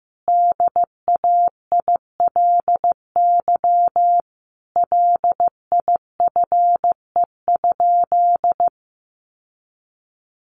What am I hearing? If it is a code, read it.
Morse code, heard as a keyed tone, about 15 wpm, 700 Hz. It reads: DAILY LIFE?
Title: Sentences from Top 1100 Words 15wpm